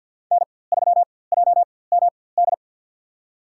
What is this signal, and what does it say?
Morse code, heard as a keyed tone, about 46 wpm, spaced as for 23 wpm, 700 Hz. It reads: N3JMD
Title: USA Call Signs - 4x Character Spacing - 50wpm